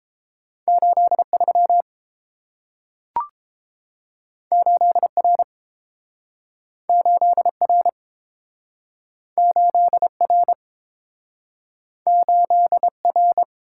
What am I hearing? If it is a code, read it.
Morse code, heard as a keyed tone, about 30 wpm, 700 Hz. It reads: 8 3 8 R 8 R 8 R 8 R